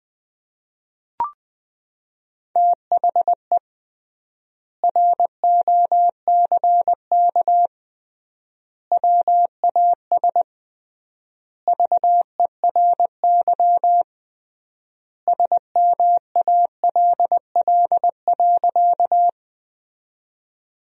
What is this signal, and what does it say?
Morse code, heard as a keyed tone, about 20 wpm, 700 Hz. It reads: THE ROCK WAS VERY SMALL.
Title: 3x Word-Spacing - Sentences from Top 400 Words - 20wpm